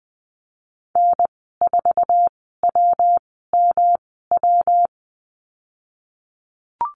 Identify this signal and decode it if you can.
Morse code, heard as a keyed tone, about 19 wpm, spaced as for 14 wpm, 700 Hz. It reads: N4WMW
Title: 147 Lesson 24g - Letter y - Call Signs 20wpm